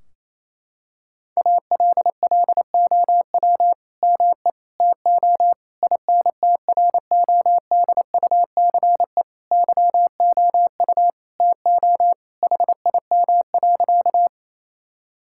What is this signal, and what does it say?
Morse code, heard as a keyed tone, about 28 wpm, 700 Hz. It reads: ALLOW ME TO INTRODUCE YOU TO HIM.
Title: Sentences from Top 1600 Words 28wpm